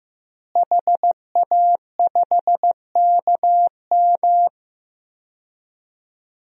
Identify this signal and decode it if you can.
Morse code, heard as a keyed tone, about 15 wpm, 700 Hz. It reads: HA5KM
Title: DXCC-100 Call Signs - 5 Characters - Repeated 2x - 15wpm